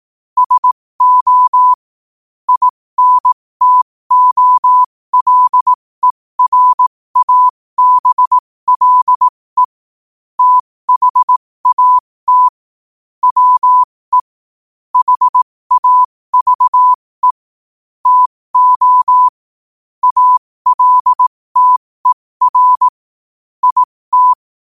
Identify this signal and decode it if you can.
Morse code, heard as a keyed tone, about 18 wpm, spaced as for 15 wpm, 1000 Hz. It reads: SO INTOLERABLE THAT WE HAVE TO ALTER IT
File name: qotdfiles / 15wpm